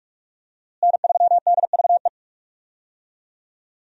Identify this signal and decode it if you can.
Morse code, heard as a keyed tone, about 46 wpm, 700 Hz. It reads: N3DVE
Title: USA Call Signs - 5 Characters - Repeated 3x - 45wpm